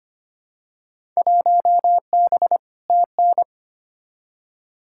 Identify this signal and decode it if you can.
Morse code, heard as a keyed tone, about 25 wpm, 700 Hz. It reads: 1B TN
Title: Field Day Exchanges - 25wpm